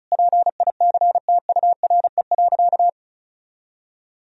Morse code, heard as a keyed tone, about 36 wpm, 700 Hz. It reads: PICTURE.
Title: Sentences from Top 1000 Words - Review Part 1 - 35wpm